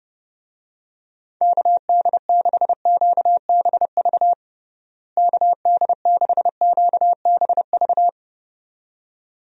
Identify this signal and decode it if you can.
Morse code, heard as a keyed tone, about 31 wpm, 700 Hz. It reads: KD6QBV KD6QBV
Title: USA Call Signs - 6 Characters - Repeated 2x - 30wpm